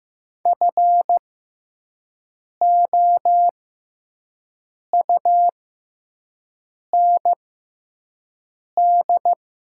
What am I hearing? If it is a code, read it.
Morse code, heard as a keyed tone, about 15 wpm, spaced as for 5 wpm, 700 Hz. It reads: FOUND
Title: 6x Character Spacing - Top 300 Words - 15wpm